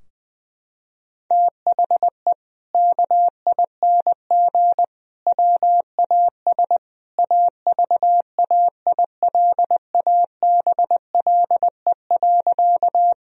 Morse code, heard as a keyed tone, about 20 wpm, 700 Hz. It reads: THE KING WAS AVAILABLE.
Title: Sentences from Top 1000 Words - Review Part 3 - 20wpm